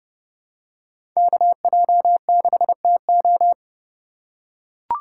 Morse code, heard as a keyed tone, about 30 wpm, 700 Hz. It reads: KJ6TO
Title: USA Call Signs - 5 Characters - 2x Character Spacing - 30wpm